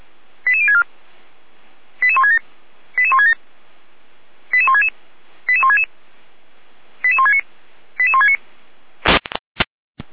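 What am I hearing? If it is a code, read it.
Five-tone selective call (ZVEI), heard as quick runs of five tones, tones of about 70 ms. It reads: 88064 88167 88167 88166 88166 88169 88169
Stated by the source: ZVEI five-tone selective call received on an SDR.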